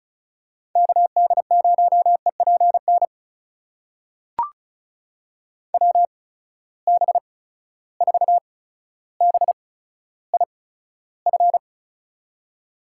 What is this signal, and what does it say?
Morse code, heard as a keyed tone, about 36 wpm, spaced as for 15 wpm, 700 Hz. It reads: KD0EPN WB4BIF
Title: USA Call Signs - 8x Character Spacing - 35wpm